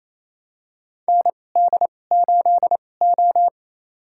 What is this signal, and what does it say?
Morse code, heard as a keyed tone, about 28 wpm, spaced as for 20 wpm, 700 Hz. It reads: ND8O
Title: USA Call Signs - 2x Character Spacing - 28wpm